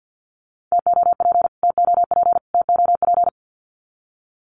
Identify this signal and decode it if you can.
Morse code, heard as a keyed tone, about 50 wpm, 700 Hz. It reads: TOP TOP TOP
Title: Top 500 Words Repeated 50wpm